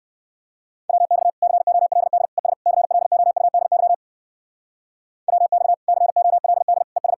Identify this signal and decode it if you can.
Morse code, heard as a keyed tone, about 56 wpm, 700 Hz. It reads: WX COLD ES CLOUDY WX COLD ES
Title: Speed-Racing Common QSO Phrases Encore - 45wpm